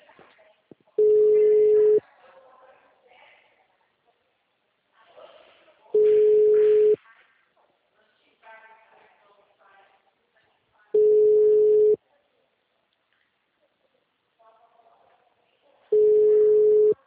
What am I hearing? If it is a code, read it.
Ringback tone (Europe, 425 Hz), with speech.
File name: out- -2016- - - .12637